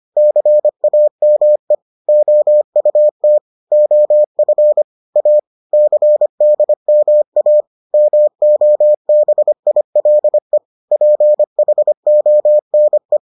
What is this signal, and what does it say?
Morse code, heard as a keyed tone, about 25 wpm, 600 Hz. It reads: CAME OUT OF A CDMA MOBILE PHONE